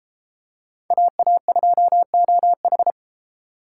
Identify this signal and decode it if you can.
Morse code, heard as a keyed tone, about 28 wpm, 700 Hz. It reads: AA2OH